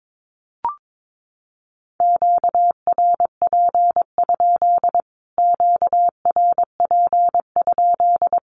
Morse code, heard as a keyed tone, about 22 wpm, 700 Hz. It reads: QRP? QRP?
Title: Q-Codes Repeated 3x 22wpm